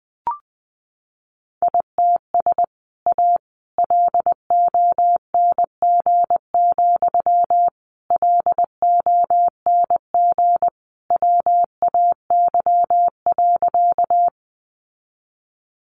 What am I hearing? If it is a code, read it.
Morse code, heard as a keyed tone, about 20 wpm, 700 Hz. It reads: ITS A LONG, LONG WAY.
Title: Sentences from Top 600 Words 20wpm